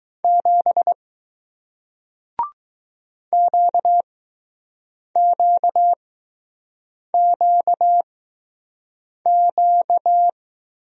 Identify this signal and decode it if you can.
Morse code, heard as a keyed tone, about 20 wpm, spaced as for 6 wpm, 700 Hz. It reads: 7 QQQQ